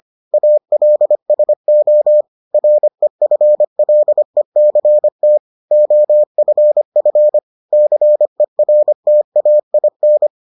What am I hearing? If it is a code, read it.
Morse code, heard as a keyed tone, about 25 wpm, 600 Hz. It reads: ALSO REFLECT OFF CERTAIN